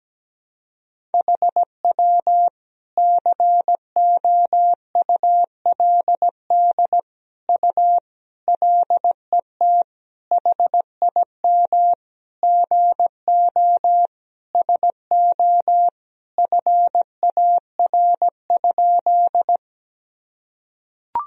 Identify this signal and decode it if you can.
Morse code, heard as a keyed tone, about 17 wpm, 700 Hz. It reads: HW COULD U LET HIM GO SO FAR?
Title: Sentences from Top 300 Words and Abbreviations 17wpm